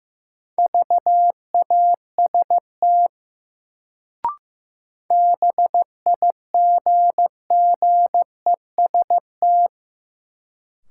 Morse code, heard as a keyed tone, about 15 wpm, 700 Hz. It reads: VAST BIGGEST
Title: Sets of 4 Words - New Word Drill - 15wpm